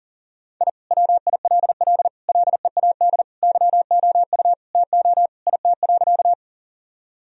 Morse code, heard as a keyed tone, about 40 wpm, 700 Hz. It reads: I WILL LEAD YOU TO IT.
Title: Sentences from Top 1000 Words - Review Part 2 - 40wpm